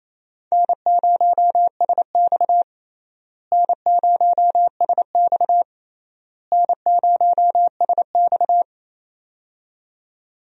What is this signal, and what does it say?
Morse code, heard as a keyed tone, about 28 wpm, 700 Hz. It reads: N0SX N0SX N0SX
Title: USA Call Signs - 4 Characters - Repeated 3x - 28wpm